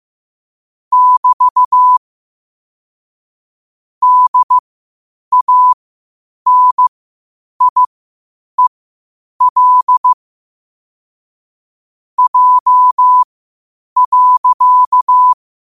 Morse code, heard as a keyed tone, about 15 wpm, spaced as for 8 wpm, 1000 Hz. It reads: <BT> DANIEL J.